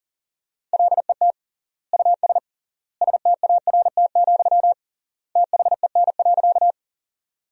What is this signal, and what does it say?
Morse code, heard as a keyed tone, about 40 wpm, 700 Hz. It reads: LET US START, THEN.
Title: 3x Word-Spacing - Sentences from Top 300 Words - 40wpm